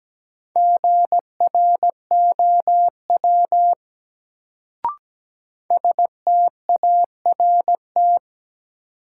Morse code, heard as a keyed tone, about 17 wpm, 700 Hz. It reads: GROW START
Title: Sentences from Top 300 Words - New Word Drill - 17wpm